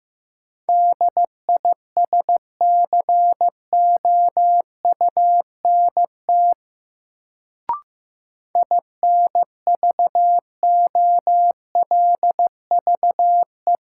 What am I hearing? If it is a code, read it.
Morse code, heard as a keyed tone, about 15 wpm, 700 Hz. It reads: DISCOUNT INVOLVE
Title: Sentences from Top 1200 Words - New Word Drill - 15wpm